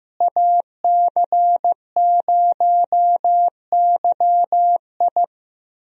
Morse code, heard as a keyed tone, about 15 wpm, 700 Hz. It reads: AC0YI
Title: USA Call Signs - 5 Characters - Repeated 2x - 15wpm